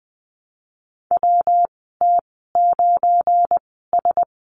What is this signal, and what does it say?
Morse code, heard as a keyed tone, about 20 wpm, spaced as for 14 wpm, 700 Hz. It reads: WT9S
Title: 186 Lesson 30g - Letter b - Call Signs 20wpm